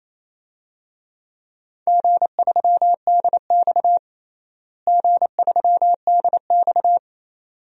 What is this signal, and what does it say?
Morse code, heard as a keyed tone, about 28 wpm, 700 Hz. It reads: G3DX G3DX
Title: DXCC-100 Call Signs - 4 Characters - Repeated 2x - 28wpm